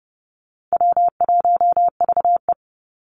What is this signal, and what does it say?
Morse code, heard as a keyed tone, about 30 wpm, 700 Hz. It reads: W1VE
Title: Call Signs - 30wpm